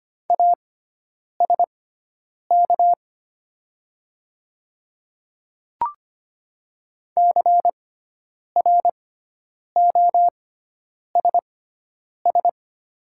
Morse code, heard as a keyed tone, about 25 wpm, spaced as for 9 wpm, 700 Hz. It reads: ASK CROSS
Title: 6x Character Spacing - Top 300 Words - 25wpm